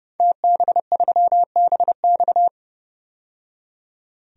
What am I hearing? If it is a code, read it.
Morse code, heard as a keyed tone, about 31 wpm, 700 Hz. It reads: TB3BX
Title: DXCC-100 Call Signs - 5 Characters - 30wpm